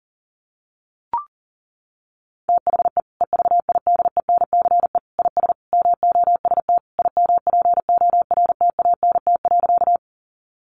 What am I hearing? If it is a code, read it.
Morse code, heard as a keyed tone, about 40 wpm, 700 Hz. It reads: THE EVIDENCE IS MOST IMPORTANT.
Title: Sentences from Top 1300 Words 40wpm